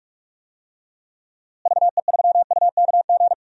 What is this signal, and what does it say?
Morse code, heard as a keyed tone, about 46 wpm, 700 Hz. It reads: VE3UKG
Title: DXCC-100 Call Signs - 6 Characters - Repeated 3x - 45wpm